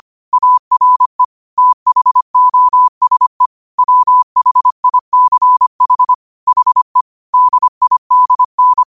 Morse code, heard as a keyed tone, about 25 wpm, 1000 Hz. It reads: ARE THOSE WHICH HE DIDN
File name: qotdfiles / 25wpm